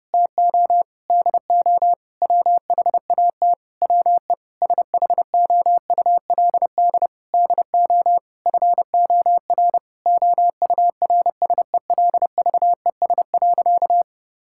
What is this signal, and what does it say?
Morse code, heard as a keyed tone, about 30 wpm, 700 Hz. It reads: TO DO WHAT WE SHOULD DO FOR OURSELVES.
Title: Sentences from Top 1600 Words 30wpm